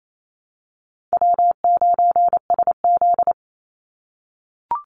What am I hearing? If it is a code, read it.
Morse code, heard as a keyed tone, about 28 wpm, 700 Hz. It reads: W9SZ
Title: Easy Call Signs - 28wpm